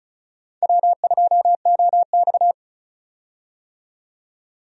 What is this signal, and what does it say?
Morse code, heard as a keyed tone, about 36 wpm, 700 Hz. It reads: W2OX
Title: USA Call Signs - 4 Characters - 35wpm